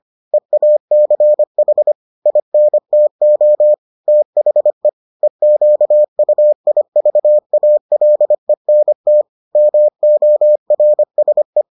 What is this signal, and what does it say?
Morse code, heard as a keyed tone, about 25 wpm, 600 Hz. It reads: EACH INTO THE EQUIVALENT MORSE